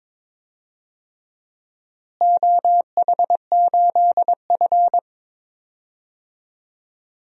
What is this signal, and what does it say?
Morse code, heard as a keyed tone, about 22 wpm, 700 Hz. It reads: OH8F
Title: DXCC-100 Call Signs - 4 Characters - Repeated 3x - 22wpm